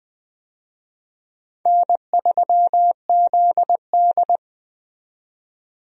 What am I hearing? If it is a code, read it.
Morse code, heard as a keyed tone, about 20 wpm, 700 Hz. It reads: N3ZD
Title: USA Call Signs - 4 Characters - 20wpm